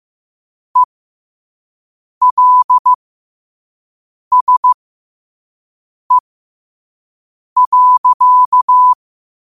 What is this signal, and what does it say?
Morse code, heard as a keyed tone, about 15 wpm, spaced as for 5 wpm, 1000 Hz. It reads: ELSE.
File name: 5wpm